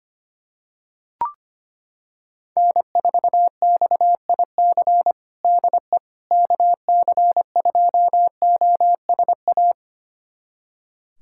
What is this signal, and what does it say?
Morse code, heard as a keyed tone, about 25 wpm, 700 Hz. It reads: N4XIC DE KC2OSA
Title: USA Callsign DE Callsign 25wpm